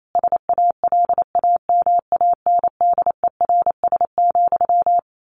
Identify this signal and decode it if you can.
Morse code, heard as a keyed tone, about 28 wpm, 700 Hz. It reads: SALAMANDERS,
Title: A Sound Of Thunder 28wpm